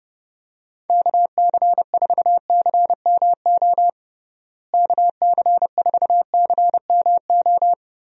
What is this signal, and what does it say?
Morse code, heard as a keyed tone, about 30 wpm, 700 Hz. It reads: KC4CMO KC4CMO
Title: USA Call Signs - 6 Characters - Repeated 3x - 30wpm